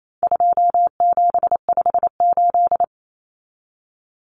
Morse code, heard as a keyed tone, about 28 wpm, 700 Hz. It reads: 2758
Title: Numbers - 4-Digit 28wpm